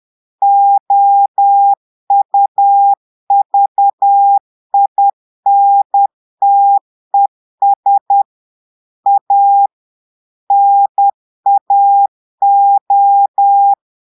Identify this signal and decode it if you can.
Morse code, heard as a keyed tone, about 10 wpm, 800 Hz. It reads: OUVINTES A NAO
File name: portugais texte en clair 2 vitesse 10 mots minutes